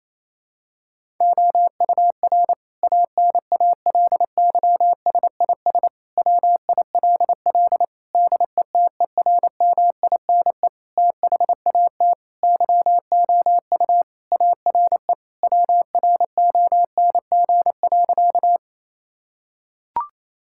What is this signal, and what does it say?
Morse code, heard as a keyed tone, about 28 wpm, 700 Hz. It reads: OUR ANALYSIS WILL DETERMINE THAT YOU ARE WRONG.
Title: Sentences from Top 1000 Words - Review Part 5 - 28wpm